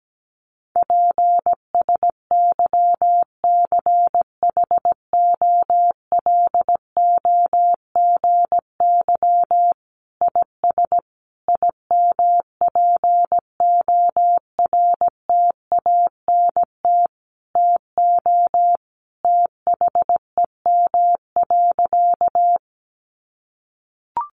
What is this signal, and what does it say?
Morse code, heard as a keyed tone, about 17 wpm, 700 Hz. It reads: PSYCHOLOGY IS IMPORTANT TO THEM.
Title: Sentences from Top 1400 Words 17wpm